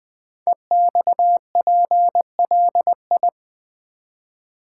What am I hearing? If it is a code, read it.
Morse code, heard as a keyed tone, about 20 wpm, 700 Hz. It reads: EXPLI
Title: Build A Word - 5 Letters At A Time - 10-Letter Words - 20wpm